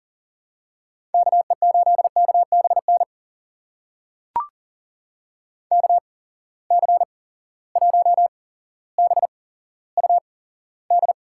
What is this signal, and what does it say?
Morse code, heard as a keyed tone, about 40 wpm, spaced as for 17 wpm, 700 Hz. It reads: KE8KBN KC1BUD
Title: USA Call Signs - 6 Characters - 8x Character Spacing - 40wpm